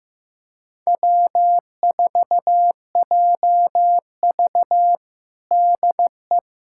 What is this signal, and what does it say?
Morse code, heard as a keyed tone, about 15 wpm, 700 Hz. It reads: W4JV DE
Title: USA Callsign DE Callsign - 4 Characters - 15wpm